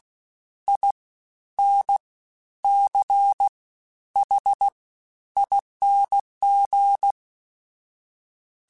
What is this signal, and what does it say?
Morse code, heard as a keyed tone, about 16 wpm, spaced as for 10 wpm, 800 Hz. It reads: INCHING
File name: Special INGs-mp3s / Special INGs-mp3s / SlowToFaster ing @15